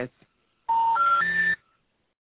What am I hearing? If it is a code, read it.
Special information tone, with speech.